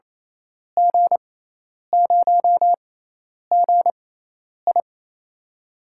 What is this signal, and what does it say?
Morse code, heard as a keyed tone, about 28 wpm, spaced as for 10 wpm, 700 Hz. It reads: G0GI
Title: USA Call Signs - 6x Character Spacing - 28wpm